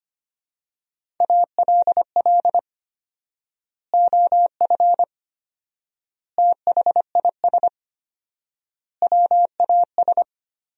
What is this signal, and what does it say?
Morse code, heard as a keyed tone, about 25 wpm, 700 Hz. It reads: ALL OF THIS WAS
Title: 4x Word-Spacing - Sentences from Top 500 Words - 25wpm